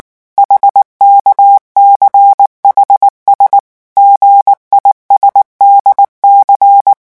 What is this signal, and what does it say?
Morse code, heard as a keyed tone, about 19 wpm, 800 Hz. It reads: HKCHS GISDC
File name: morse mp3 / revision7-vitesse 18 mots